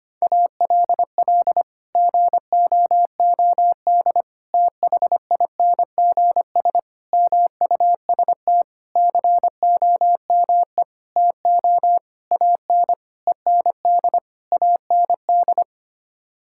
Morse code, heard as a keyed tone, about 25 wpm, 700 Hz. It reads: ALL GOOD THINGS MUST COME TO AN END AND
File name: Sets of 9 Words - 25wpm